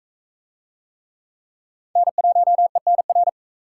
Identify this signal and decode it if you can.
Morse code, heard as a keyed tone, about 40 wpm, 700 Hz. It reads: N1ENR